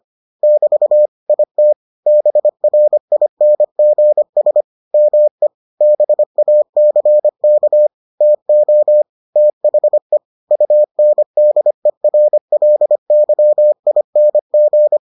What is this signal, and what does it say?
Morse code, heard as a keyed tone, about 25 wpm, 600 Hz. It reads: <BT> IT BRINGS ME BACK TO THE UNDERLYING